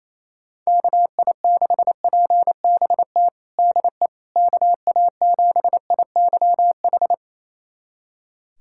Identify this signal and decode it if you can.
Morse code, heard as a keyed tone, about 28 wpm, 700 Hz. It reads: KI6PBT DE KA7IYH
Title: USA Callsign DE Callsign - 6 Characters - 28wpm